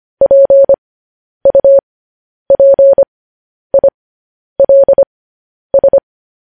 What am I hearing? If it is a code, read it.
Morse code, heard as a keyed tone, about 25 wpm, spaced as for 10 wpm, 550 Hz. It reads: PUPILS
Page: news headlines in morse code at 10 wpm